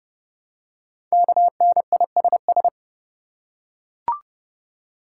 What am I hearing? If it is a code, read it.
Morse code, heard as a keyed tone, about 29 wpm, 700 Hz. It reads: KNISS
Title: Code Groups - 5 Letters - 30wpm